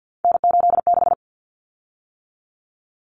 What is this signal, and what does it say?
Morse code, heard as a keyed tone, about 50 wpm, 700 Hz. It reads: N86
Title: Common Cut Numbers - 3-Digit 50wpm